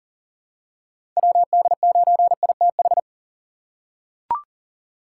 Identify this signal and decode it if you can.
Morse code, heard as a keyed tone, about 40 wpm, 700 Hz. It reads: WD9ITH
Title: USA Call Signs - 4x Character Spacing - 40wpm